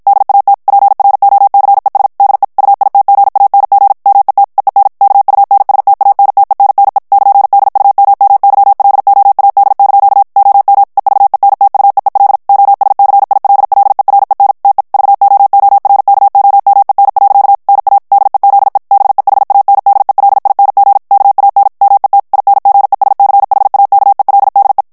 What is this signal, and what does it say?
Morse code, heard as a keyed tone, about 53 wpm, 800 Hz. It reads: DAT PROCES DE UITGANG MET EEN KUNSTANTENNE <KN>DUMMYLOAD) OM EVENTUEEL OSCILLEREN TE VOORKOMEN. NA DEZE BEHANDELING KAN MET INGESCHAKELDE
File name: HOLLANDAIS texte clair 2 vitesse 50 mots minute